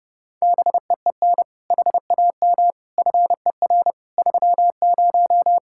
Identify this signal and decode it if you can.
Morse code, heard as a keyed tone, about 30 wpm, 700 Hz. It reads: BEEN HAM FER 30